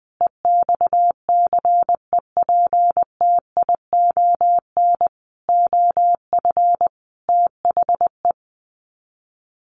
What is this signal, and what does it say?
Morse code, heard as a keyed tone, about 20 wpm, 700 Hz. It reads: EXCEPTION OF THE
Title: Sets of 5 Words Encore 20wpm